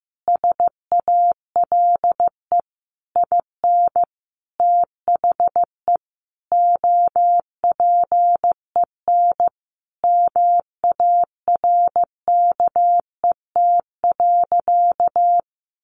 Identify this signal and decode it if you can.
Morse code, heard as a keyed tone, about 15 wpm, 700 Hz. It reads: SALE IN THE OPEN MARKET.
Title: Sentences from Top 1400 Words 15wpm